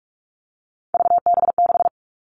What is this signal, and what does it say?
Morse code, heard as a keyed tone, about 44 wpm, 700 Hz. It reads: VB6
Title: All Cut Numbers - 3-Digit 45wpm